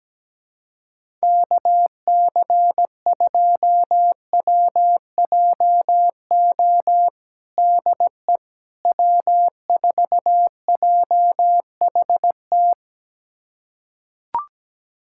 Morse code, heard as a keyed tone, about 17 wpm, 700 Hz. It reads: KC2WJO DE W4JHT
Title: USA Callsign DE Callsign 17wpm